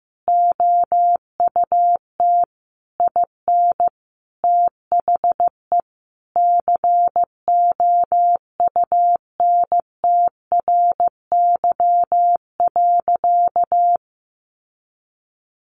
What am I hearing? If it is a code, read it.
Morse code, heard as a keyed tone, about 15 wpm, 700 Hz. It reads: OUT IN THE COUNTRY.
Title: Sentences from Top 700 Words 15wpm